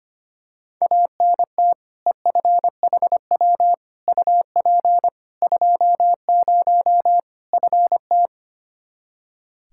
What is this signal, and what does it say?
Morse code, heard as a keyed tone, about 25 wpm, 700 Hz. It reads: ANT EFHW UP 20 FT
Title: Practice QSO - Exchange 2 - Info - 25wpm